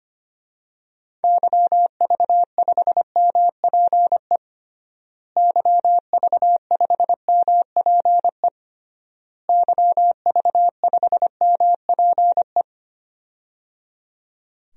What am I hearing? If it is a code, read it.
Morse code, heard as a keyed tone, about 25 wpm, 700 Hz. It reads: YV5MPE YV5MPE YV5MPE
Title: DXCC-100 Call Signs - 6 Characters - Repeated 3x - 25wpm